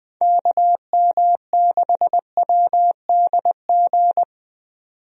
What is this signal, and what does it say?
Morse code, heard as a keyed tone, about 20 wpm, 700 Hz. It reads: KM6WDG
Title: USA Call Signs - 6 Characters - 20wpm